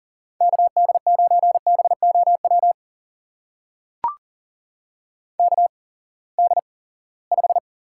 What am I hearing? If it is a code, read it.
Morse code, heard as a keyed tone, about 42 wpm, spaced as for 23 wpm, 700 Hz. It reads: KD9BOW KD5
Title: USA Call Signs - 6 Characters - 8x Character Spacing - 40wpm